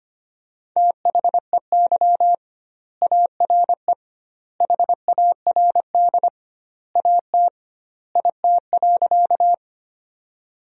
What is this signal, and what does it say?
Morse code, heard as a keyed tone, about 25 wpm, 700 Hz. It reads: THEY ARE HARD AT IT.
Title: 2x Word-Spacing - Sentences from Top 300 Words - 25wpm